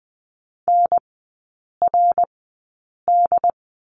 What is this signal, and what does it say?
Morse code, heard as a keyed tone, about 20 wpm, spaced as for 8 wpm, 700 Hz. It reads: NRD
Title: 172 Lesson 28e - Number 9 - Three Characters 20wpm